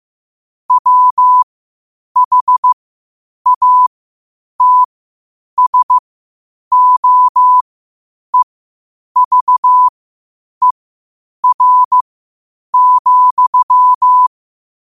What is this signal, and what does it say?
Morse code, heard as a keyed tone, about 15 wpm, spaced as for 8 wpm, 1000 Hz. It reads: WHATSOEVER,
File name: qotdfiles / 8wpm